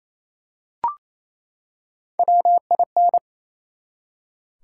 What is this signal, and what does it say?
Morse code, heard as a keyed tone, about 28 wpm, 700 Hz. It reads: WIN
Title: Three Letter Words - 28wpm